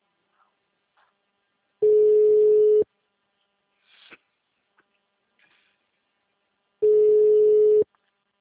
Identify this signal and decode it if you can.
Ringback tone (Europe, 425 Hz).
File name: out- -2002- - - .72